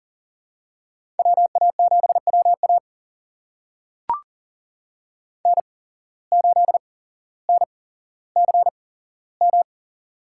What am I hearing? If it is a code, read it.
Morse code, heard as a keyed tone, about 40 wpm, spaced as for 17 wpm, 700 Hz. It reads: WA7WA N8NCM
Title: USA Call Signs - 5 Characters - 8x Character Spacing - 40wpm